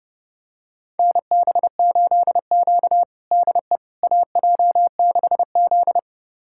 Morse code, heard as a keyed tone, about 32 wpm, 700 Hz. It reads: NB8Q DE AJ6Z
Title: USA Callsign DE Callsign - 4 Characters - 30wpm